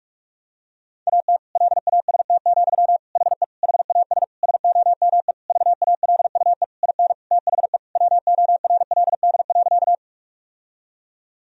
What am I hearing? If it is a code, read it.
Morse code, heard as a keyed tone, about 47 wpm, 700 Hz. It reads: AT LAST, HE HAS SOME VALUE IN THE WORLD.
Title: Sentences from Top 1000 Words - Review Part 3 - 45wpm